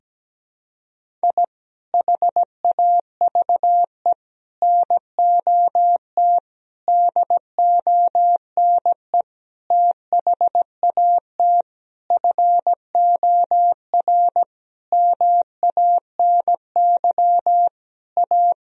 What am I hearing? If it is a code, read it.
Morse code, heard as a keyed tone, about 17 wpm, 700 Hz. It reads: I HAVE NOT DONE THAT FOR MANY A
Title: Sentences from Top 1000 Words - Review Part 1 - 17wpm